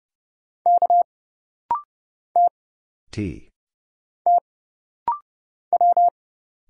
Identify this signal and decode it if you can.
Morse code, heard as a keyed tone, about 29 wpm, 700 Hz. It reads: K T T W
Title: Single Letter-Number - Rapid-Fire 30wpm